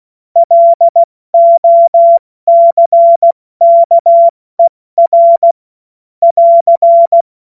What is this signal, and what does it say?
Morse code, heard as a keyed tone, about 16 wpm, spaced as for 14 wpm, 680 Hz. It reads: LOCKER <AR>
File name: Morsekurs / Witze / Witz 07. September 2021, 70 F80